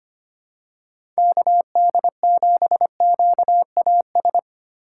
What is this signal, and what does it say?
Morse code, heard as a keyed tone, about 25 wpm, 700 Hz. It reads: KD7QAS
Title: USA Call Signs - 6 Characters - 2x Character Spacing - 25wpm